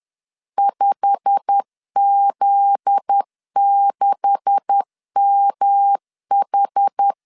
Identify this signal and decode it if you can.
Morse code, heard as a keyed tone, about 10 wpm, 800 Hz. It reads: 5Z6MH